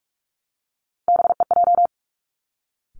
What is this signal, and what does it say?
Morse code, heard as a keyed tone, about 46 wpm, 700 Hz. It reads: BEJ
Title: Three Challenging Letter-Number Combinations 45wpm